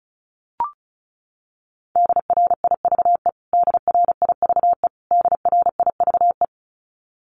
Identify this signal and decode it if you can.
Morse code, heard as a keyed tone, about 34 wpm, 700 Hz. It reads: DRIVE DRIVE DRIVE
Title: Top 500 Words Repeated 35wpm